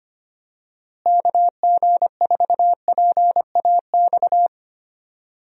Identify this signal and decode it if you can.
Morse code, heard as a keyed tone, about 25 wpm, 700 Hz. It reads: KG4PAX
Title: USA Call Signs - 6 Characters - Repeated 2x - 25wpm